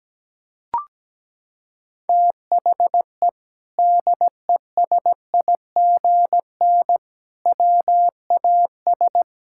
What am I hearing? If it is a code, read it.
Morse code, heard as a keyed tone, about 17 wpm, 700 Hz. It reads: THE DESIGN WAS
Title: Sentences from Top 1000 Words - Review Part 5 - 17wpm